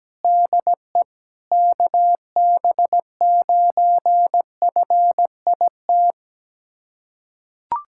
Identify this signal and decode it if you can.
Morse code, heard as a keyed tone, about 17 wpm, 700 Hz. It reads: DE KB9FIT
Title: USA Callsign DE Callsign 17wpm